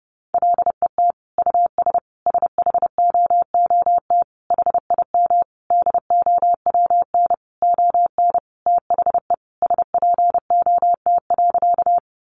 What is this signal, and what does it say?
Morse code, heard as a keyed tone, about 30 wpm, 700 Hz. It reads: LET US SHOOT HIM DOWN ON THE SPOT.
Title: Sentences from Top 1200 Words 30wpm